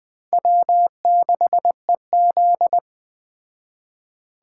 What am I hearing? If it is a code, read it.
Morse code, heard as a keyed tone, about 21 wpm, 700 Hz. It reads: W6EZ